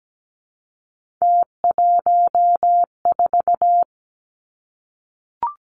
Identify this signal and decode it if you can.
Morse code, heard as a keyed tone, about 17 wpm, 700 Hz. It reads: T14
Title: Three Challenging Letter-Number Combinations 17wpm